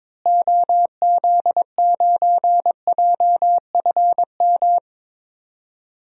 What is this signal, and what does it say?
Morse code, heard as a keyed tone, about 22 wpm, 700 Hz. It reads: OZ9JFM
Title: DXCC-100 Call Signs - 6 Characters - Repeated 2x - 22wpm